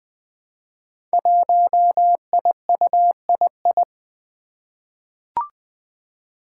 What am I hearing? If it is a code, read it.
Morse code, heard as a keyed tone, about 20 wpm, 700 Hz. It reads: 1IUII
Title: Five Challenging Letter-Number Combinations 20wpm